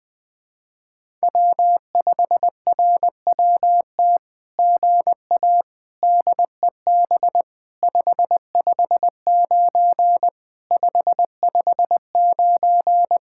Morse code, heard as a keyed tone, about 20 wpm, 700 Hz. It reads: W5RWT GA DEB 559 559
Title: SOTA QSO Practice 20wpm